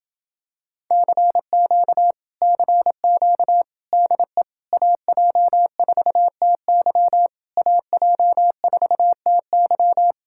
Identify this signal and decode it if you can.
Morse code, heard as a keyed tone, about 28 wpm, 700 Hz. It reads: CQ CQ DE AJ4TY AJ4TY